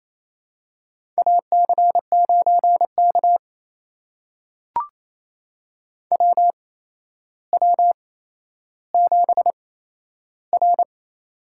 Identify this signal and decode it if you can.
Morse code, heard as a keyed tone, about 28 wpm, spaced as for 12 wpm, 700 Hz. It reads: AC9K WW7R
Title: USA Call Signs - 8x Character Spacing - 28wpm